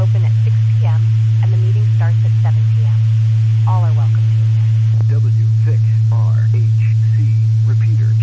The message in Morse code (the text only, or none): none